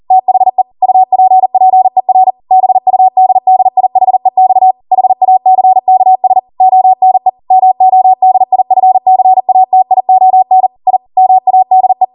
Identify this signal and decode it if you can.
Morse code, heard as a keyed tone, about 40 wpm, 750 Hz. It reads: THE UPPER BUDDIHEX HACKS ONE MODIFCATION I MADE
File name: Morse / 40WPM